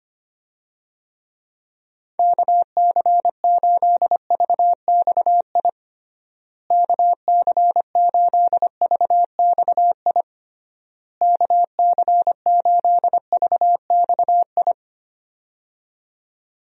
Morse code, heard as a keyed tone, about 25 wpm, 700 Hz. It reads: KC8VXI KC8VXI KC8VXI
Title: USA Call Signs - 6 Characters - Repeated 3x - 25wpm